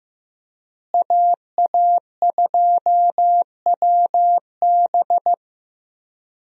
Morse code, heard as a keyed tone, about 15 wpm, 700 Hz. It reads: AA2WB
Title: USA Call Signs - 5 Characters - Repeated 3x - 15wpm